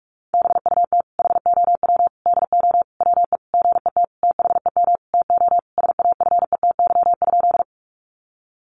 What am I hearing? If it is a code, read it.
Morse code, heard as a keyed tone, about 44 wpm, 700 Hz. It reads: BUT HOW DO WE GET THEM TO SAFETY?
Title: Sentences from Top 700 Words 45wpm